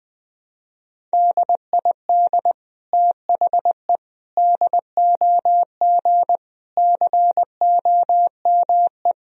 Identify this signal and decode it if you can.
Morse code, heard as a keyed tone, about 20 wpm, 700 Hz. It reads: DID THE DOG COME
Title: Sentences from Top 1000 Words - Review Part 2 - 20wpm